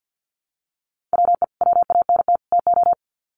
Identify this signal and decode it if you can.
Morse code, heard as a keyed tone, about 50 wpm, 700 Hz. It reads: WE WANT TO
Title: Sets of 3 Words 50wpm